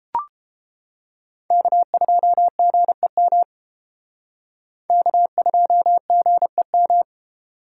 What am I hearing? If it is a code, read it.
Morse code, heard as a keyed tone, about 30 wpm, 700 Hz. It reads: K2GEM K2GEM